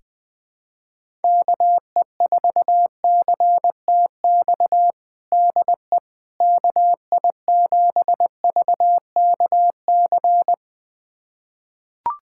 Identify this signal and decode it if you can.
Morse code, heard as a keyed tone, about 20 wpm, 700 Hz. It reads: KE4CTX DE KI7VKC
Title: USA Callsign DE Callsign - 6 Characters - 20wpm